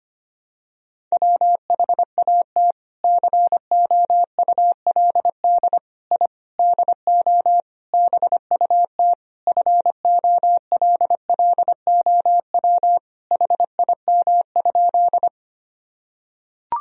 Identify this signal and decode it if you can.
Morse code, heard as a keyed tone, about 25 wpm, 700 Hz. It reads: WHAT COULD I DO BUT FOLLOW HIM?
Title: Sentences from Top 1000 Words - Review Part 1 - 25wpm